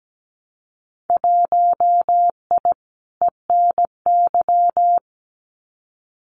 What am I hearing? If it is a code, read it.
Morse code, heard as a keyed tone, about 17 wpm, 700 Hz. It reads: 1I ENY
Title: Winter Field Day Exchanges Encore 17wpm